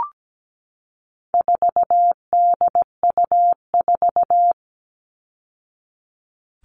Morse code, heard as a keyed tone, about 17 wpm, 700 Hz. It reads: 4DU4
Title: All Cut Numbers - 4-Digit 17wpm